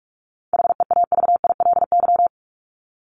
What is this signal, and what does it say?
Morse code, heard as a keyed tone, about 44 wpm, 700 Hz. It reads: HEAVILY
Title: LY Suffix - Top 100 Words with Base Words 45wpm